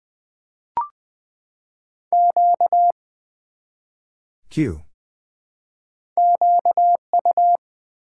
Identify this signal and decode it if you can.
Morse code, heard as a keyed tone, about 20 wpm, 700 Hz. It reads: Q QU
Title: Build A Word - 1 Letter At A Time - 5-Letter Words - 20wpm